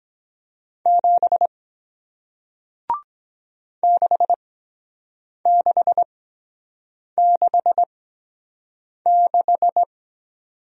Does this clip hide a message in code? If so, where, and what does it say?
Yes: Morse code from 0.9 to 9.8 s, heard as a keyed tone, about 23 wpm, spaced as for 7 wpm, 700 Hz. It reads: 7 6666